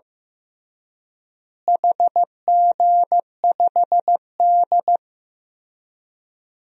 Morse code, heard as a keyed tone, about 15 wpm, 700 Hz. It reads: HG5D
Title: DXCC-100 Call Signs - 4 Characters - Repeated 3x - 15wpm